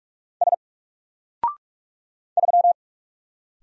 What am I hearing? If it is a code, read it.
Morse code, heard as a keyed tone, about 48 wpm, 700 Hz. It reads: S 3
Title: Single Letter-Number Alt 45wpm